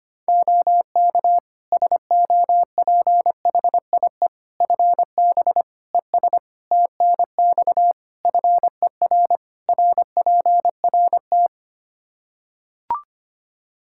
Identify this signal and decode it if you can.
Morse code, heard as a keyed tone, about 25 wpm, 700 Hz. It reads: OK SOPHIE FB ES TNX FER RPRT
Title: Practice QSO - Exchange 2 - Info - 25wpm